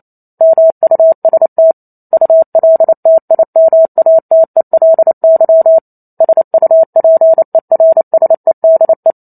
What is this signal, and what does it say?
Morse code, heard as a keyed tone, about 28 wpm, 650 Hz. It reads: MUST ULTIMATELY SUPERSEDE